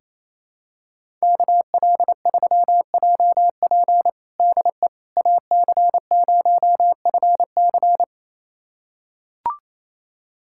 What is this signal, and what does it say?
Morse code, heard as a keyed tone, about 28 wpm, 700 Hz. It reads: KL3JP DE AC0FC
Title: USA Callsign DE Callsign - 5 Characters - 28wpm